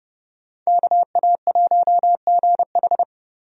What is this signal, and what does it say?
Morse code, heard as a keyed tone, about 32 wpm, 700 Hz. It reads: KA1GH
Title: USA Call Signs - 5 Characters - Repeated 3x - 30wpm